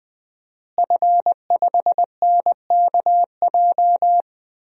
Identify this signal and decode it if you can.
Morse code, heard as a keyed tone, about 20 wpm, 700 Hz. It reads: F5NKJ
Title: DXCC-100 Call Signs - 5 Characters - Repeated 3x - 20wpm